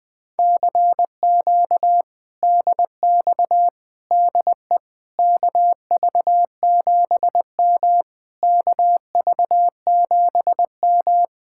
Morse code, heard as a keyed tone, about 21 wpm, 700 Hz. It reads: CQ DX DE KV7M KV7M
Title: Speed-Racing Common QSO Phrases - 20wpm